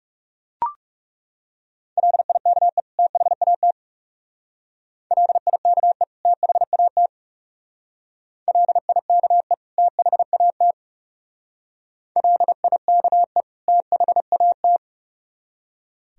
Morse code, heard as a keyed tone, about 35 wpm, 700 Hz. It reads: LIKE THAT LIKE THAT LIKE THAT LIKE THAT
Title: Speed-Racing - Sets of 2 Words 30wpm